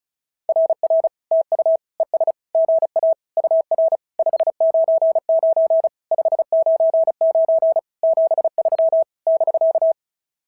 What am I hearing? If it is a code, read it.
Morse code, heard as a keyed tone, about 36 wpm, 650 Hz. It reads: RR TU ES GA UR 599 599 73 <BK>
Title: SOTA QSO Practice 35wpm